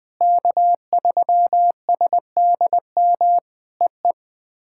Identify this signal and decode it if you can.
Morse code, heard as a keyed tone, about 20 wpm, 700 Hz. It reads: K3SDM EE
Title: POTA QSO Practice - 20wpm